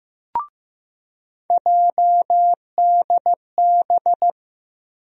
Morse code, heard as a keyed tone, about 15 wpm, 700 Hz. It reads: JDB
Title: USA Call Sign Postfixes - 3 Letters 15wpm